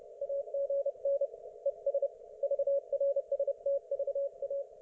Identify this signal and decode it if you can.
Morse code, heard as a keyed tone, about 30 wpm, 560 Hz. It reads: AGN ES VRSTVA
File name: Archiv / CUC208 / ok1fhi cul208